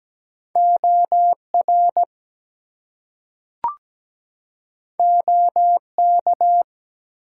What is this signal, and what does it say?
Morse code, heard as a keyed tone, about 17 wpm, 700 Hz. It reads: OR OK
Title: Two Letter Words - 17wpm